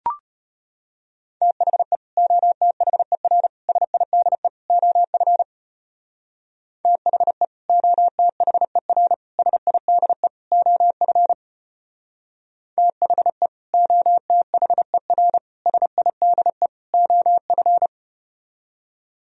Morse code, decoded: THE OTHER SIDE OF THE OTHER SIDE OF THE OTHER SIDE OF

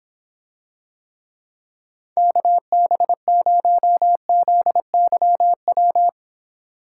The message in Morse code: KB0ZYW

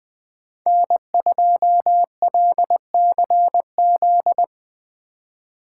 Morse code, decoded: N2LCZ

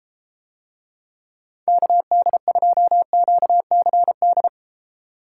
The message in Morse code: KD2QCD

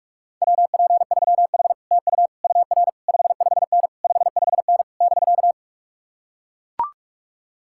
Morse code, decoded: WP3H TU UR 55N 55N <BK>

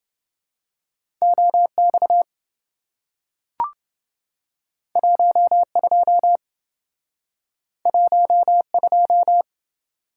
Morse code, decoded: OX 12 12

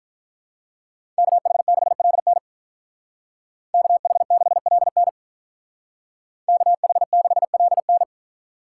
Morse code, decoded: KH6LN KH6LN KH6LN

